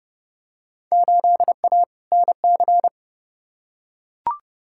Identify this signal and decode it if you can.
Morse code, heard as a keyed tone, about 29 wpm, 700 Hz. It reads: 8A NC
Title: Field Day Exchanges - 30wpm